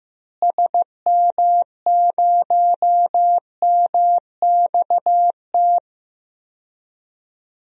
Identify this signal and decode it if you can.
Morse code, heard as a keyed tone, about 15 wpm, 700 Hz. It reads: SM0MXT
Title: DXCC-100 Call Signs - 6 Characters - Repeated 3x - 15wpm